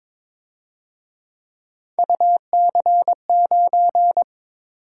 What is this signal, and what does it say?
Morse code, heard as a keyed tone, about 26 wpm, 700 Hz. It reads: UC9
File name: Speed-Racing - Three Letter-Number Combinations - 22wpm